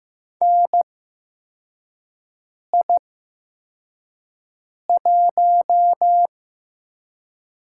Morse code, heard as a keyed tone, about 15 wpm, spaced as for 4 wpm, 700 Hz. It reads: NI1
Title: USA Call Signs - 4 Characters - 8x Character Spacing - 15wpm